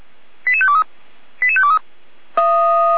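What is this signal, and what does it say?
Five-tone selective call (ZVEI), heard as quick runs of five tones, tones of about 70 ms. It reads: 88523 88523
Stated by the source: Five-tone ZVEI selective call received by SDR.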